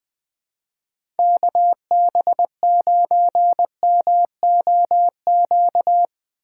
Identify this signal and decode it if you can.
Morse code, heard as a keyed tone, about 21 wpm, 700 Hz. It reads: KB9MOQ
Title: USA Call Signs - 6 Characters - 4x Character Spacing - 20wpm